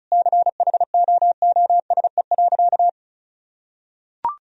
Morse code, heard as a keyed tone, about 34 wpm, 700 Hz. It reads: CHOOSE.
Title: Sentences from Top 1000 Words - Review Part 4 - 35wpm